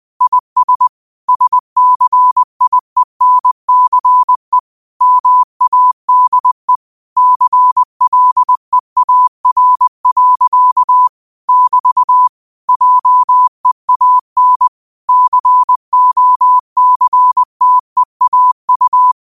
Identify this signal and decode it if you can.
Morse code, heard as a keyed tone, about 20 wpm, 1000 Hz. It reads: IS SCIENCE MADE CLEAR. <BT> JEAN COCTEAU